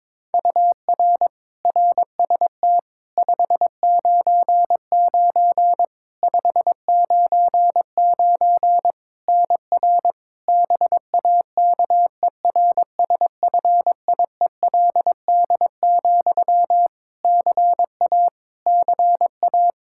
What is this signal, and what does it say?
Morse code, heard as a keyed tone, about 22 wpm, 700 Hz. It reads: UR RST 599 599 NR BAKERSFIELD, CA CA
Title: Practice QSO - Exchange 1 - Beginning - 22wpm